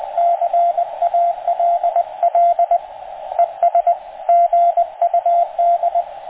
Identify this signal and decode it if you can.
Morse code, heard as a keyed tone, about 20 wpm, 690 Hz, with radio noise. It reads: CALL ES GUD